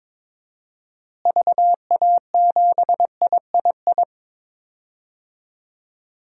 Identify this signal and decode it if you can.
Morse code, heard as a keyed tone, about 22 wpm, 700 Hz. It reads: VA7III
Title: DXCC-100 Call Signs - 6 Characters - Repeated 3x - 22wpm